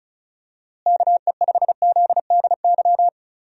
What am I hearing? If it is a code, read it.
Morse code, heard as a keyed tone, about 34 wpm, 700 Hz. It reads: KE5ZDY
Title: USA Call Signs - 6 Characters - Repeated 3x - 35wpm